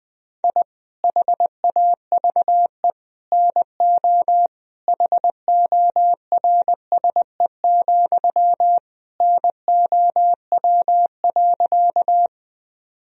Morse code, heard as a keyed tone, about 20 wpm, 700 Hz. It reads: I HAVE NO HORSE, NOW.
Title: Sentences from Top 1000 Words - Review Part 1 - 20wpm